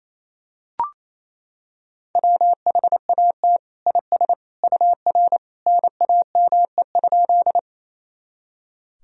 Morse code, heard as a keyed tone, about 28 wpm, 700 Hz. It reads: WHAT IS UR NAME?
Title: Sentences from Top 200 Words and Abbreviations 28wpm